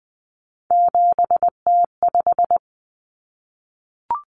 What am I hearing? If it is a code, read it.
Morse code, heard as a keyed tone, about 20 wpm, 700 Hz. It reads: 7T5
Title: Common Cut Numbers - 3-Digit 20wpm